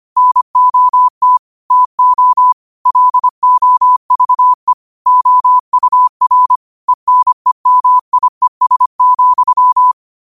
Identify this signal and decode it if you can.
Morse code, heard as a keyed tone, about 25 wpm, 1000 Hz. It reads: NOT TO LOVE OUR ENEMIES,
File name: qotdfiles / 25wpm